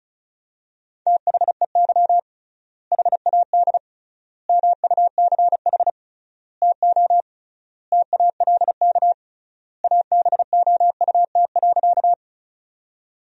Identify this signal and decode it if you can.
Morse code, heard as a keyed tone, about 36 wpm, 700 Hz. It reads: THEY HAD MUCH TO TALK ABOUT.
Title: 3x Word-Spacing - Sentences from Top 400 Words - 35wpm